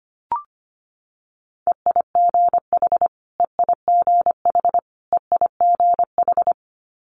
Morse code, heard as a keyed tone, about 25 wpm, 700 Hz. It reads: EIGH EIGH EIGH
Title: Multi-letter Phonemes Repeated 3x 25wpm